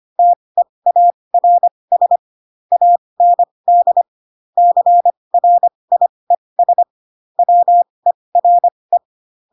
Morse code, heard as a keyed tone, about 25 wpm, spaced as for 20 wpm, 700 Hz. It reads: TEARS AND CRIES WERE
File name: Pinocchio / Pinocchio ch7 15 25 ik1hge